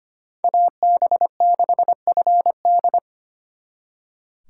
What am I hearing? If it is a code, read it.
Morse code, heard as a keyed tone, about 33 wpm, 700 Hz. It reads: AB6FD